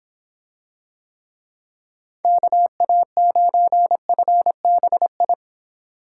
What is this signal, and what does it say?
Morse code, heard as a keyed tone, about 25 wpm, 700 Hz. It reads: KA9FBI